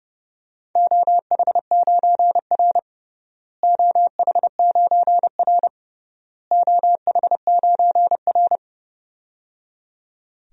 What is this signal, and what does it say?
Morse code, heard as a keyed tone, about 30 wpm, 700 Hz. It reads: OH9R OH9R OH9R
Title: DXCC-100 Call Signs - 4 Characters - Repeated 3x - 30wpm